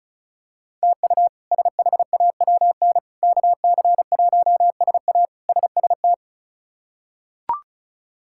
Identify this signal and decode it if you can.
Morse code, heard as a keyed tone, about 36 wpm, 700 Hz. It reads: TU SHAWN KC1SA SST
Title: K1USN SST QSO Practice - 35wpm